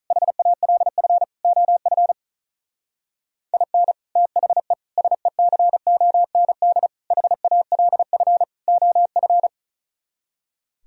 Morse code, heard as a keyed tone, about 36 wpm, 700 Hz. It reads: HALF OF IN THE SECOND HALF OF
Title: Speed-Racing - Sets of 5 Words 35wpm